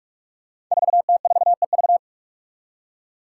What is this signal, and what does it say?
Morse code, heard as a keyed tone, about 44 wpm, 700 Hz. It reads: 4T4EV
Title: Five Challenging Letter-Number Combinations 45wpm